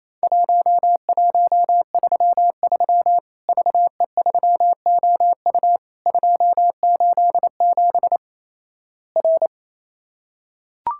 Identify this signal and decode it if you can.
Morse code, heard as a keyed tone, about 28 wpm, 700 Hz. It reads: 1133 VE3OU 287 R
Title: Worked All Europe DX Contest - QTC Exchange - 28wpm